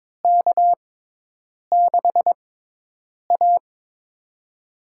Morse code, heard as a keyed tone, about 22 wpm, spaced as for 8 wpm, 700 Hz. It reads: K6A
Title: USA Call Signs - 4 Characters - 6x Character Spacing - 22wpm